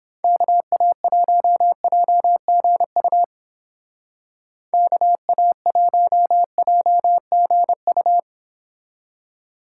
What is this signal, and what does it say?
Morse code, heard as a keyed tone, about 30 wpm, 700 Hz. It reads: KA1JGU KA1JGU